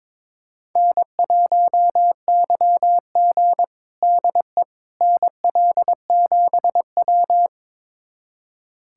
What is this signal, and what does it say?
Morse code, heard as a keyed tone, about 22 wpm, 700 Hz. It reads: N1YG DE NL7W
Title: USA Callsign DE Callsign - 4 Characters - 22wpm